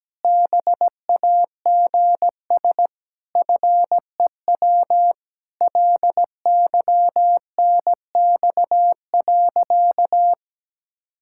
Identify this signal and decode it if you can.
Morse code, heard as a keyed tone, about 17 wpm, 700 Hz. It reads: BAGS FEW LYNX.
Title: Pangram Challenge v2 - 17wpm